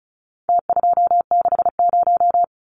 Morse code, heard as a keyed tone, about 34 wpm, 700 Hz. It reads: T260
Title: Numbers - Contest Style - 4-Digit 35wpm